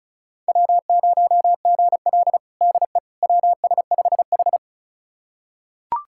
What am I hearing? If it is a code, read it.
Morse code, heard as a keyed tone, about 36 wpm, 700 Hz. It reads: W0GL DE WS5H
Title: USA Callsign DE Callsign - 4 Characters - 35wpm